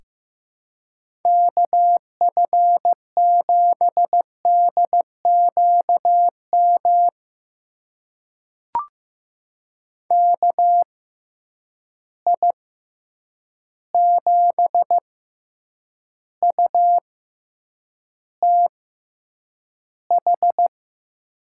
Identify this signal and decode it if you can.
Morse code, heard as a keyed tone, about 15 wpm, spaced as for 8 wpm, 700 Hz. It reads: KF7DQM KI7UTH
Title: USA Call Signs - 6 Characters - 6x Character Spacing - 15wpm